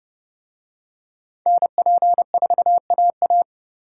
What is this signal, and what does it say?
Morse code, heard as a keyed tone, about 28 wpm, 700 Hz. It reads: NP4AA